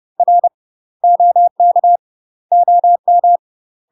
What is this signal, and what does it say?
Morse code, heard as a keyed tone, about 30 wpm, 700 Hz. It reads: R OK OM
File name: UK MASTER QSOs TRAINING / 30WPM / UK MASTER QSOs 0028